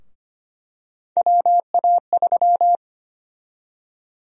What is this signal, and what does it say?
Morse code, heard as a keyed tone, about 25 wpm, 700 Hz. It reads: WA3
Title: USA Call Sign Prefixes - Repeated 2x - 25wpm